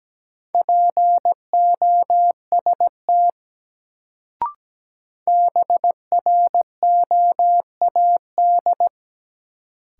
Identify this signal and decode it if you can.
Morse code, heard as a keyed tone, about 17 wpm, 700 Hz. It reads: POST BROAD
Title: Sentences from Top 1000 Words - New Word Drill - 17wpm